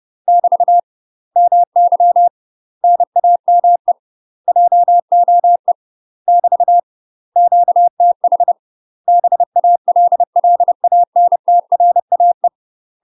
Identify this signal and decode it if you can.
Morse code, heard as a keyed tone, about 30 wpm, 700 Hz. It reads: <BT> MY NAME JOE <BT> QTH BALLANTRAE
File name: UK MASTER QSOs TRAINING/30WPM